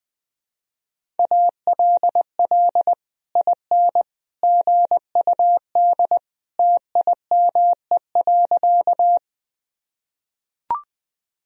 Morse code, heard as a keyed tone, about 20 wpm, 700 Hz. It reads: ALL IN GUD TIME.
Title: Sentences from Top 200 Words and Abbreviations 20wpm